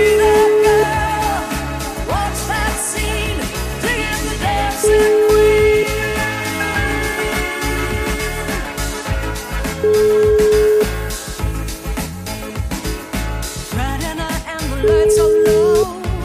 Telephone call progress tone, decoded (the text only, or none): ringback tone